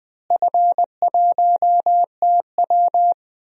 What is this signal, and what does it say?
Morse code, heard as a keyed tone, about 20 wpm, 700 Hz. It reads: F1TW
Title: DXCC-100 Call Signs - 4 Characters - Repeated 3x - 20wpm